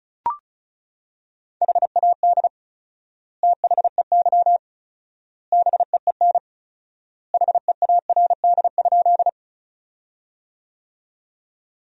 Morse code, heard as a keyed tone, about 36 wpm, 700 Hz. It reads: HAD THEY BEEN HEARD?
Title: 4x Word-Spacing - Sentences from Top 400 Words - 35wpm